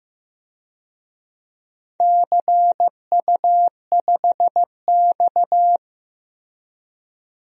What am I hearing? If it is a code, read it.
Morse code, heard as a keyed tone, about 15 wpm, 700 Hz. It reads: CU5X
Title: DXCC-100 Call Signs - 4 Characters - Repeated 2x - 15wpm